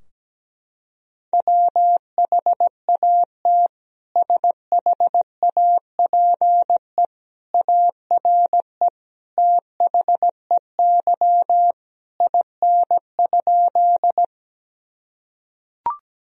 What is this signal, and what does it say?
Morse code, heard as a keyed tone, about 17 wpm, 700 Hz. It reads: WHAT SHAPE ARE THEY IN?
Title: Sentences from Top 1000 Words - Review Part 3 - 17wpm